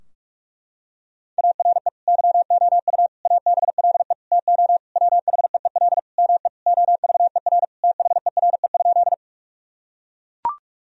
Morse code, heard as a keyed tone, about 41 wpm, 700 Hz. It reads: ARE YOU ABLE TO WHEEL ME OVER THERE?
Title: Sentences from Top 1000 Words - Review Part 3 - 45wpm